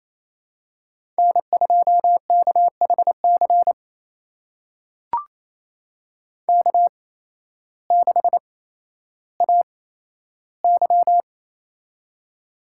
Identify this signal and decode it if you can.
Morse code, heard as a keyed tone, about 28 wpm, spaced as for 13 wpm, 700 Hz. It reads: N2KHC K6AY
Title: USA Call Signs - 8x Character Spacing - 28wpm